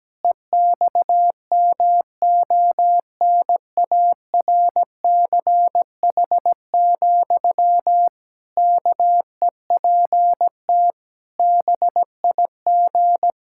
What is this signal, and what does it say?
Morse code, heard as a keyed tone, about 17 wpm, 700 Hz. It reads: EXMONARCH, KEPT BIG
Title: Pangram Challenge v2 - 17wpm